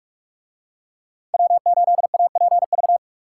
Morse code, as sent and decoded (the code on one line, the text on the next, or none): .-- ---.. .- .--. ...-
W8APV